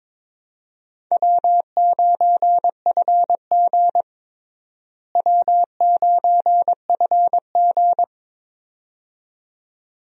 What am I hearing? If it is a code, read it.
Morse code, heard as a keyed tone, about 22 wpm, 700 Hz. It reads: W9FG W9FG
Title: USA Call Signs - 4 Characters - Repeated 2x - 22wpm